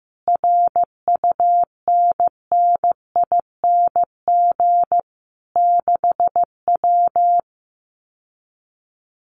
Morse code, heard as a keyed tone, about 15 wpm, 700 Hz. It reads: RUNNING 6W
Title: Common QSO Phrases Encore 15wpm